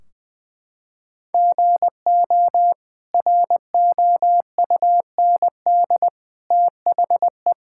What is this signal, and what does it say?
Morse code, heard as a keyed tone, about 20 wpm, 700 Hz. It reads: GO ROUND THE
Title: Sentences from Top 1600 Words 20wpm